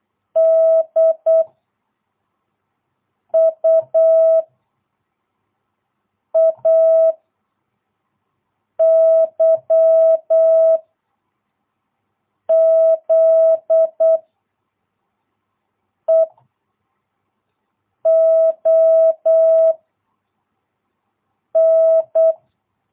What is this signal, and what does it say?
Morse code, heard as a keyed tone, about 8 wpm, 660 Hz. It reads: D U A Y Z E O N
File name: jh8cbh / 2022cw